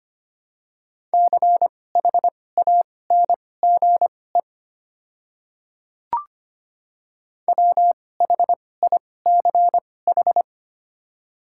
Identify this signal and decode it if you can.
Morse code, heard as a keyed tone, about 25 wpm, spaced as for 18 wpm, 700 Hz. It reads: CHANGE WHICH
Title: 2x Character Spacing - Top 500 Words - 25wpm